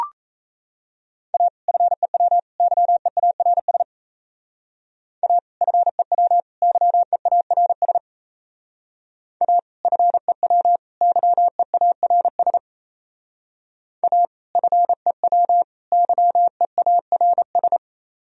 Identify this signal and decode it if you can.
Morse code, heard as a keyed tone, about 37 wpm, 700 Hz. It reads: A FEW YEARS A FEW YEARS A FEW YEARS A FEW YEARS